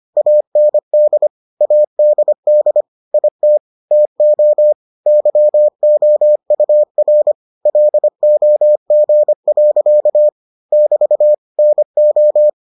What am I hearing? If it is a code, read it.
Morse code, heard as a keyed tone, about 25 wpm, 600 Hz. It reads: AND ADD IT TO YOUR LOG. <BT> NO